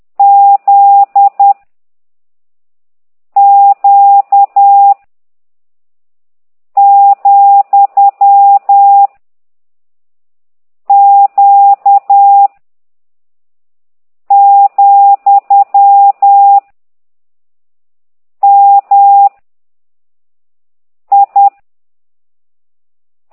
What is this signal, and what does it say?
Morse code, heard as a keyed tone, about 10 wpm, 800 Hz. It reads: Z Q , Q , M I
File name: ct2idw / exerc13